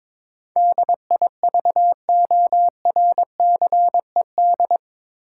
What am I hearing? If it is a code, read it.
Morse code, heard as a keyed tone, about 22 wpm, 700 Hz. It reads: DIVORCED
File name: practice / 22wpm / Eight Letter Words - 22wpm